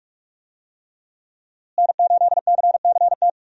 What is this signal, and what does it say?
Morse code, heard as a keyed tone, about 46 wpm, 700 Hz. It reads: N8CCT
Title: USA Call Signs - 5 Characters - Repeated 3x - 45wpm